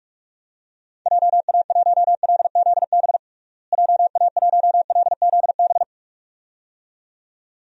Morse code, heard as a keyed tone, about 46 wpm, 700 Hz. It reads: JA1LZB JA1LZB
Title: DXCC-100 Call Signs - 6 Characters - Repeated 2x - 45wpm